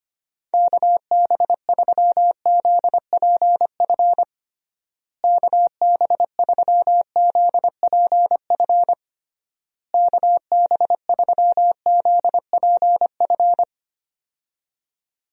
Morse code, heard as a keyed tone, about 25 wpm, 700 Hz. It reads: KB3ZPF KB3ZPF KB3ZPF
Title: USA Call Signs - 6 Characters - Repeated 3x - 25wpm